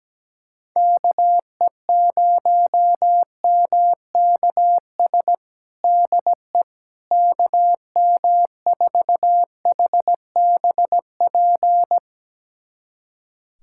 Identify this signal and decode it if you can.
Morse code, heard as a keyed tone, about 17 wpm, 700 Hz. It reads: KE0MKS DE KM4HBP
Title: USA Callsign DE Callsign - 6 Characters - 17wpm